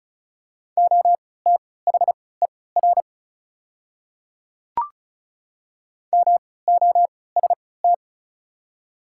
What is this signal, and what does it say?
Morse code, heard as a keyed tone, about 36 wpm, spaced as for 20 wpm, 690 Hz. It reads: OTHER MOST
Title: 3x Character Spacing - Top 100 Words - 35wpm